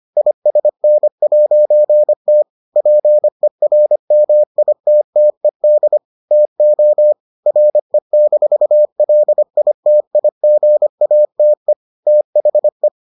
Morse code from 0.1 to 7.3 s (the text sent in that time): ISN'T PERMITTED TO